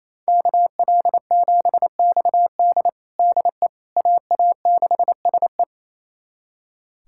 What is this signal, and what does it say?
Morse code, heard as a keyed tone, about 28 wpm, 700 Hz. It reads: KL7XD DE AA6SE
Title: USA Callsign DE Callsign - 5 Characters - 28wpm